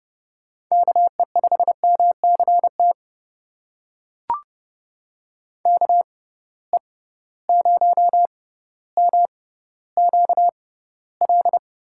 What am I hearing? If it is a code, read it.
Morse code, heard as a keyed tone, about 30 wpm, spaced as for 15 wpm, 700 Hz. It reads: KE5MCT KE0MQL
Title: USA Call Signs - 6 Characters - 6x Character Spacing - 30wpm